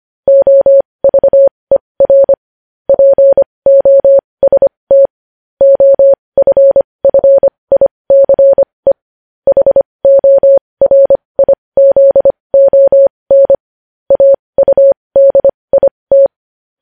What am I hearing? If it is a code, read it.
Morse code, heard as a keyed tone, about 25 wpm, spaced as for 20 wpm, 550 Hz. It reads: OVER POST OFFICE HORIZON AUDIT
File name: podcast / News headlines-1x-25x20